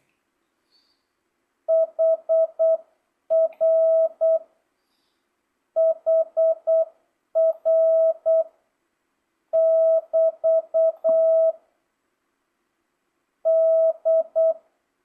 Morse code, heard as a keyed tone, about 8 wpm, 650 Hz. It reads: HR HR <BT> D